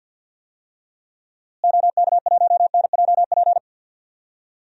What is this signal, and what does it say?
Morse code, heard as a keyed tone, about 52 wpm, 700 Hz. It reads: OK1NJP
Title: DXCC-100 Call Signs - 6 Characters - Repeated 3x - 50wpm